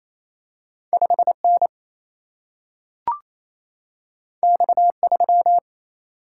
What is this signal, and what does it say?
Morse code, heard as a keyed tone, about 28 wpm, 700 Hz. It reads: 5 N X 3
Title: Single-Letter Single-Number 28wpm